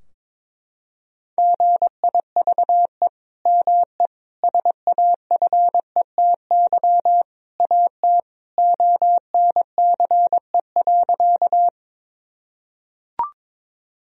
Morse code, heard as a keyed tone, about 22 wpm, 700 Hz. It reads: GIVE ME SAFETY AT ONCE.
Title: Sentences from Top 1000 Words - Review Part 4 - 22wpm